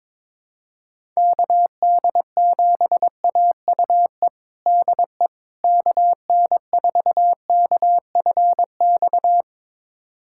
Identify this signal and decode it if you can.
Morse code, heard as a keyed tone, about 22 wpm, 700 Hz. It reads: KD7AUE DE KN4KFX
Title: USA Callsign DE Callsign - 6 Characters - 22wpm